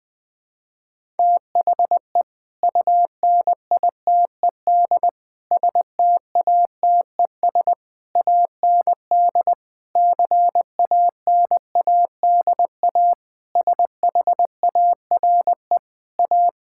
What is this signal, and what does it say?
Morse code, heard as a keyed tone, about 21 wpm, 700 Hz. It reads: THE UNITED STATES AND CANADA SHARE A
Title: Sentences with 5-Word Phrases - 20wpm